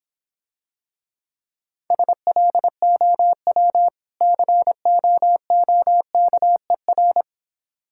Morse code, heard as a keyed tone, about 26 wpm, 700 Hz. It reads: SLOW COOKER